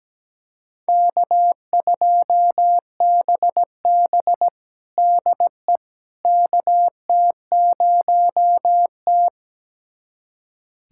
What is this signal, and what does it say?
Morse code, heard as a keyed tone, about 17 wpm, 700 Hz. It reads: K2BB DE KT0T
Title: USA Callsign DE Callsign - 4 Characters - 17wpm